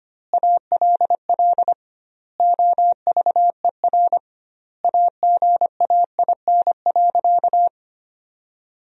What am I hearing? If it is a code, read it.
Morse code, heard as a keyed tone, about 25 wpm, 700 Hz. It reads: ALL OVER AGAIN.
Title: 2x Word-Spacing - Sentences from Top 500 Words - 25wpm